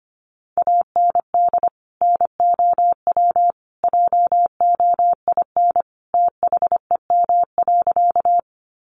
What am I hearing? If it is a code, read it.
Morse code, heard as a keyed tone, about 25 wpm, 700 Hz. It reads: AND NOW JOIN THEM.
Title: Sentences from Top 1400 Words 25wpm